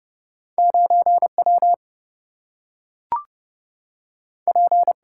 Morse code, heard as a keyed tone, about 30 wpm, 700 Hz. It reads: 9W P
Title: Single-Letter Single-Number 30wpm